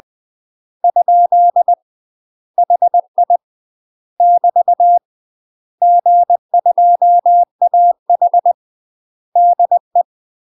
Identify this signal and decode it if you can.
Morse code, heard as a keyed tone, about 20 wpm, 700 Hz. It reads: ? HI <BT> G2AH DE
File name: UK MASTER QSOs TRAINING / 20WPM / UK MASTER QSOs 0032